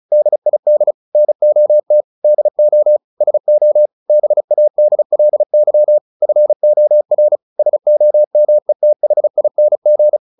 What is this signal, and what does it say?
Morse code, heard as a keyed tone, about 34 wpm, 600 Hz. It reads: DID NOT DO SO BADLY FOR SOMETHING
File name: the old man and the sea 2 35wpm